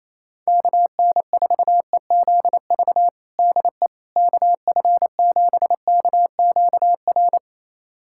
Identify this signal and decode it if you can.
Morse code, heard as a keyed tone, about 28 wpm, 700 Hz. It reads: KN4EZV DE KF7KQR
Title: USA Callsign DE Callsign 28wpm